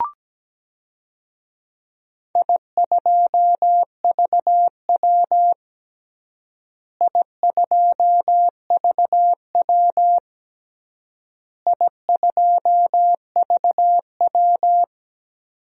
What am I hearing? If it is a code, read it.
Morse code, heard as a keyed tone, about 17 wpm, 700 Hz. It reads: I2VW I2VW I2VW
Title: DXCC-100 Call Signs - 4 Characters - Repeated 3x - 17wpm